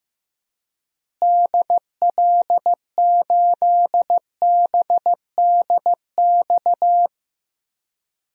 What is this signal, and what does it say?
Morse code, heard as a keyed tone, about 15 wpm, 700 Hz. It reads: DL8BDX
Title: DXCC-100 Call Signs - 6 Characters - Repeated 3x - 15wpm